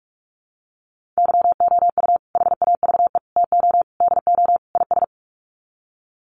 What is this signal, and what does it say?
Morse code, heard as a keyed tone, about 44 wpm, 700 Hz. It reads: YOU HAVE TO DO IS
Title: Sets of 5 Words 45wpm